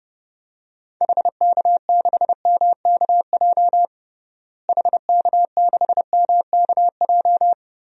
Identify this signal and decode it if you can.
Morse code, heard as a keyed tone, about 31 wpm, 700 Hz. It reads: HK6MKJ HK6MKJ
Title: DXCC-100 Call Signs - 6 Characters - Repeated 3x - 30wpm